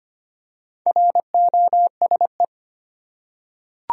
Morse code, heard as a keyed tone, about 25 wpm, 700 Hz. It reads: ROSE
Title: USA Common Names - Max 4 Letters - 25wpm